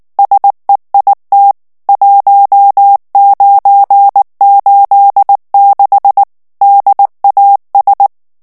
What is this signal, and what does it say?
Morse code, heard as a keyed tone, about 19 wpm, 800 Hz. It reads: SEIT 1986 DAS
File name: morse mp3 / ALLEMAND texte clair 2 vitesse 18 mots minute